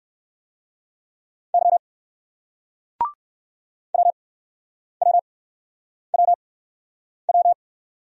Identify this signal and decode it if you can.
Morse code, heard as a keyed tone, about 60 wpm, spaced as for 9 wpm, 700 Hz. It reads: Y WWWW